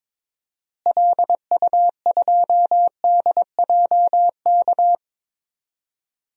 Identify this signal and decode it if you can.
Morse code, heard as a keyed tone, about 22 wpm, 700 Hz. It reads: LU2DJK
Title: DXCC-100 Call Signs - 6 Characters - Repeated 3x - 22wpm